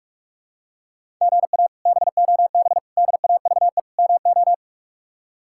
Morse code, heard as a keyed tone, about 46 wpm, 700 Hz. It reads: GA BOB DAVE MO
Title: K1USN SST QSO Practice - 45wpm